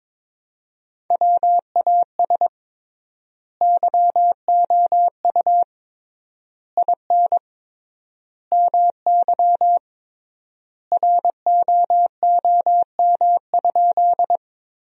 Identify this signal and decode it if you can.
Morse code, heard as a keyed tone, about 22 wpm, 700 Hz. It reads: WAS YOU IN MY ROOM?
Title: 3x Word-Spacing - Sentences from Top 300 Words - 22wpm